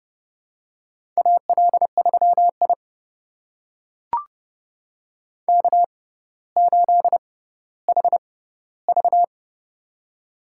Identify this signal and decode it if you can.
Morse code, heard as a keyed tone, about 31 wpm, spaced as for 15 wpm, 700 Hz. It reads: AL3I K8HV
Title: USA Call Signs - 4 Characters - 6x Character Spacing - 30wpm